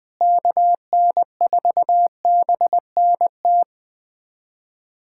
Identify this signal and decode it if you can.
Morse code, heard as a keyed tone, about 20 wpm, 700 Hz. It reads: KN4BNT
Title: USA Callsign DE Callsign - 6 Characters - 20wpm